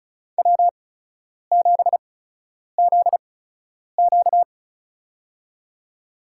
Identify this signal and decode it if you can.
Morse code, heard as a keyed tone, about 36 wpm, spaced as for 10 wpm, 700 Hz. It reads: W7ZQ
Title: USA Call Signs - 8x Character Spacing - 35wpm